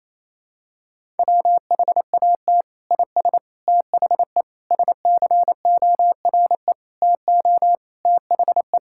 Morse code, heard as a keyed tone, about 28 wpm, 700 Hz. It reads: WHAT IS THE SCORE TO THE
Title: Sentences from Top 1600 Words 28wpm